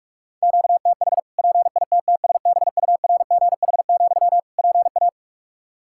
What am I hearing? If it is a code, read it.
Morse code, heard as a keyed tone, about 46 wpm, 700 Hz. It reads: QTH PITTSBURGH, PA
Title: Practice QSO - Exchange 1 - Beginning - 45wpm